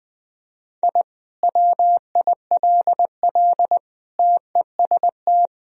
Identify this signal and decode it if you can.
Morse code, heard as a keyed tone, about 20 wpm, 700 Hz. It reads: I WILL TEST
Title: Sentences from Top 1000 Words - Review Part 3 - 20wpm